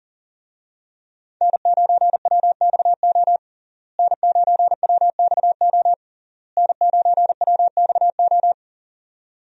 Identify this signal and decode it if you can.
Morse code, heard as a keyed tone, about 40 wpm, 700 Hz. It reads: N9WXO N9WXO N9WXO
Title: USA Call Signs - 5 Characters - Repeated 3x - 40wpm